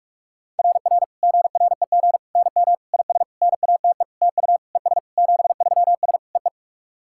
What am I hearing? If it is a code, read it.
Morse code, heard as a keyed tone, about 41 wpm, 690 Hz. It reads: RR GREG NM IS NATE TU ES 73S EE
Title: SOTA QSO Practice 45wpm